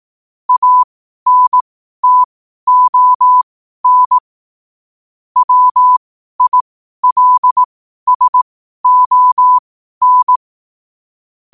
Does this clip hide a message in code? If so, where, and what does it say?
Yes: Morse code from 0.5 to 10.4 s, heard as a keyed tone, about 18 wpm, spaced as for 12 wpm, 1000 Hz. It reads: ANTON WILSON